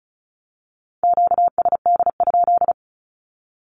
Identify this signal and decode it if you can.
Morse code, heard as a keyed tone, about 34 wpm, 700 Hz. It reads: QSD?
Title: Q-Codes 35wpm